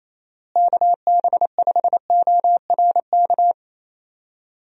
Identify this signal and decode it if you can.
Morse code, heard as a keyed tone, about 27 wpm, 700 Hz. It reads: KB5ORK
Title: USA Call Signs - 6 Characters - Repeated 2x - 28wpm